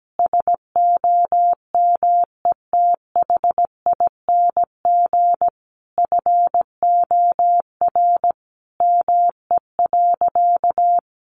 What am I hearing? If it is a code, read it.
Morse code, heard as a keyed tone, about 17 wpm, 700 Hz. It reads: SOMETHING FOR ME.
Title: Sentences from Top 600 Words 17wpm